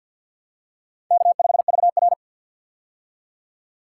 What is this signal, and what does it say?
Morse code, heard as a keyed tone, about 49 wpm, 700 Hz. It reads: K5VR
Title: K1USN SST QSO Practice - 50wpm